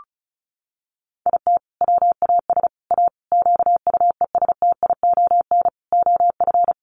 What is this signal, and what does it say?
Morse code, heard as a keyed tone, about 34 wpm, 700 Hz. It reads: IT WAS A QUESTION OF
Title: Sentences from Top 1400 Words 35wpm